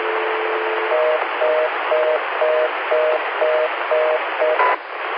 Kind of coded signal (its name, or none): telephone call progress tone